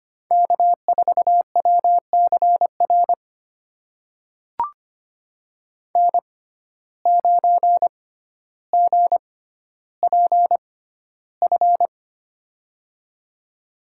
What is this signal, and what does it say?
Morse code, heard as a keyed tone, about 25 wpm, spaced as for 13 wpm, 700 Hz. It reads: K4WCR N9GPF
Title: USA Call Signs - 5 Characters - 6x Character Spacing - 25wpm